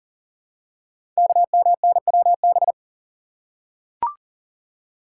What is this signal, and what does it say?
Morse code, heard as a keyed tone, about 40 wpm, 700 Hz. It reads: KMNWB
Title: Code Groups - 5 Letters - 40wpm